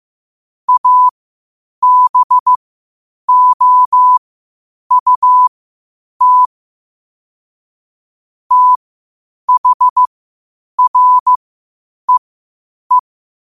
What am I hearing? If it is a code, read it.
Morse code, heard as a keyed tone, about 15 wpm, spaced as for 8 wpm, 1000 Hz. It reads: ABOUT THREE
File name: qotdfiles / 8wpm